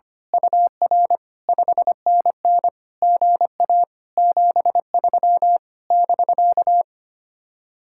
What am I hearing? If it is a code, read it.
Morse code, heard as a keyed tone, about 24 wpm, 700 Hz. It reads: UR 5NN GA 73 <BK>
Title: Speed-Racing Common QSO Phrases Encore - 25wpm